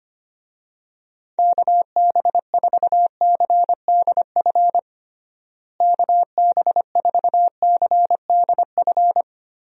Morse code, heard as a keyed tone, about 25 wpm, 700 Hz. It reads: KB4CDF KB4CDF
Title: USA Call Signs - 6 Characters - Repeated 2x - 25wpm